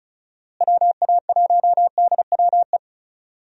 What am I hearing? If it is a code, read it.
Morse code, heard as a keyed tone, about 36 wpm, 700 Hz. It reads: WA1DWE
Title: USA Call Signs - 6 Characters - Repeated 3x - 35wpm